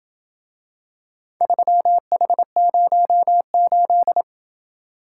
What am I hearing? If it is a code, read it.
Morse code, heard as a keyed tone, about 26 wpm, 700 Hz. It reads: 3H08